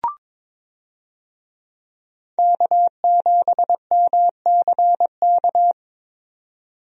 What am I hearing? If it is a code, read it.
Morse code, heard as a keyed tone, about 22 wpm, 700 Hz. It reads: K7MCK
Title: USA Call Signs - 5 Characters - 22wpm